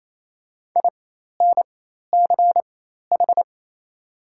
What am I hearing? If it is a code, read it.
Morse code, heard as a keyed tone, about 28 wpm, spaced as for 13 wpm, 700 Hz. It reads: INCH
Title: 4x Character Spacing - Top 500 Words - 28wpm